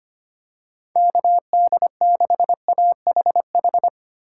Morse code, heard as a keyed tone, about 25 wpm, 700 Hz. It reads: KD6AHH
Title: USA Call Signs - 6 Characters - 3x Character Spacing - 25wpm